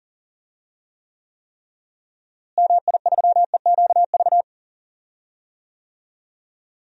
Morse code, heard as a keyed tone, about 38 wpm, 700 Hz. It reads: MI3EQV
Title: DXCC-100 Call Signs - 6 Characters - Repeated 3x - 40wpm